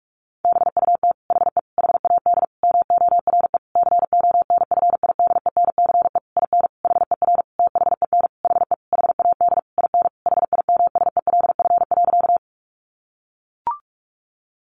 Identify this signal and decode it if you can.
Morse code, heard as a keyed tone, about 44 wpm, 700 Hz. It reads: BUT HE HAD MORE CONFIDENCE IN HER THEN HE HAD IN HIMSELF.
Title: Sentences from Top 1500 Words 45wpm